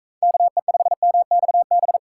Morse code, heard as a keyed tone, about 42 wpm, 700 Hz. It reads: KE5MXB